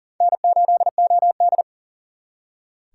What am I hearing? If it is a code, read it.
Morse code, heard as a keyed tone, about 52 wpm, 700 Hz. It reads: N8OD